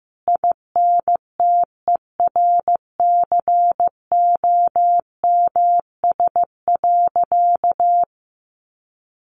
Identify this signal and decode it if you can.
Morse code, heard as a keyed tone, about 15 wpm, 700 Hz. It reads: INTERCOMS.
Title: A Sound Of Thunder 15wpm